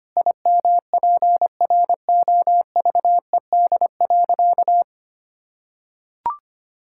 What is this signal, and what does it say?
Morse code, heard as a keyed tone, about 25 wpm, 700 Hz. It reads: IMPROVED.
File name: practice / 25wpm / Sentences with 4-Word Phrases - 25wpm